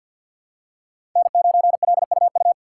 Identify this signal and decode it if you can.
Morse code, heard as a keyed tone, about 53 wpm, 700 Hz. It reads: N9LUU
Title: USA Call Signs - 5 Characters - 6x Character Spacing - 50wpm